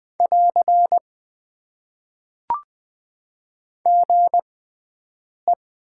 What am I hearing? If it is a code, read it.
Morse code, heard as a keyed tone, about 20 wpm, spaced as for 7 wpm, 700 Hz. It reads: <AR> GE
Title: Top 90 QSO Elements - 6x Character Spacing - 20wpm